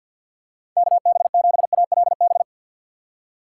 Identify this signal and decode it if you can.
Morse code, heard as a keyed tone, about 51 wpm, 700 Hz. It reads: KB7ALB
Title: USA Call Signs - 6 Characters - 3x Character Spacing - 50wpm